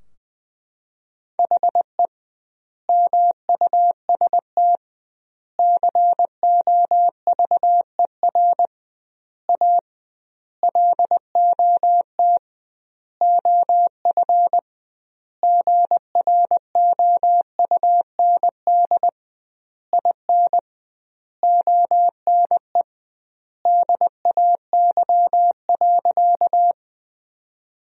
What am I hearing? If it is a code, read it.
Morse code, heard as a keyed tone, about 20 wpm, 700 Hz. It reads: HE MUST COVER A LOT OF GROUND IN ONE DAY.
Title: 2x Word-Spacing - Sentences from Top 500 Words - 20wpm